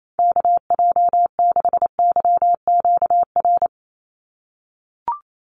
Morse code, heard as a keyed tone, about 28 wpm, 700 Hz. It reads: KJ6YQR
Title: USA Call Signs 28wpm